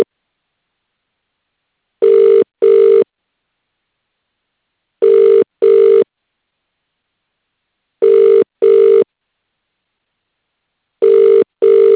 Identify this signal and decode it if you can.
Ringback tone (UK), 400 + 450 Hz.